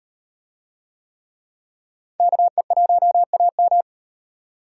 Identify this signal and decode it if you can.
Morse code, heard as a keyed tone, about 37 wpm, 700 Hz. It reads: KE1AM